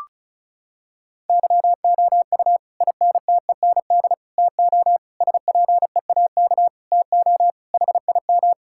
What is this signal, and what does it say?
Morse code, heard as a keyed tone, about 36 wpm, 700 Hz. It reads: YOU INTEND TO SPEAK TO HIM
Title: Sentences from Top 1600 Words 35wpm